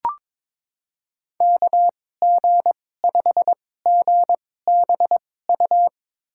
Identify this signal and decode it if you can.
Morse code, heard as a keyed tone, about 22 wpm, spaced as for 16 wpm, 700 Hz. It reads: KG5GBU
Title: USA Call Signs - 6 Characters - 2x Character Spacing - 22wpm